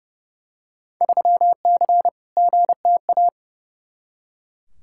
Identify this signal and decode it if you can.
Morse code, heard as a keyed tone, about 32 wpm, 700 Hz. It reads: 3C GTA
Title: Field Day Exchanges Encore - 30wpm